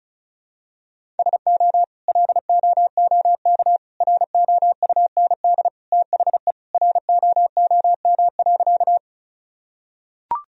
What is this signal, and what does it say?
Morse code, heard as a keyed tone, about 36 wpm, 700 Hz. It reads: SO LOOK ROUND THE ROOM.
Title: Sentences from Top 1000 Words - Review Part 1 - 35wpm